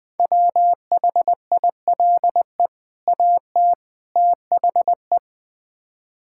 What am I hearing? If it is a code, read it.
Morse code, heard as a keyed tone, about 20 wpm, 700 Hz. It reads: WHILE AT THE
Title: Build A Phrase - 5-Word Phrases - 20wpm